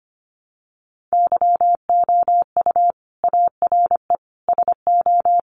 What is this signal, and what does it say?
Morse code, heard as a keyed tone, about 25 wpm, 700 Hz. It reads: YOU ARE SO
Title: Sentences from Top 1200 Words 25wpm